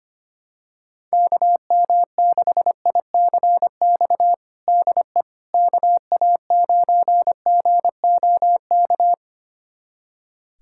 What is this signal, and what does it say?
Morse code, heard as a keyed tone, about 25 wpm, 700 Hz. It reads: KM6ICX DE KA9GOK
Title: USA Callsign DE Callsign - 6 Characters - 25wpm